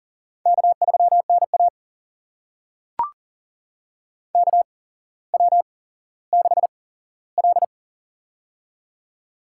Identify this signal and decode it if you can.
Morse code, heard as a keyed tone, about 40 wpm, spaced as for 17 wpm, 700 Hz. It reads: K3NA KW6L
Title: USA Call Signs - 4 Characters - 8x Character Spacing - 40wpm